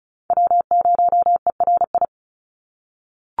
Morse code, heard as a keyed tone, about 34 wpm, 700 Hz. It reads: W0ERI
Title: NA SOTA Chaser Callsigns 35wpm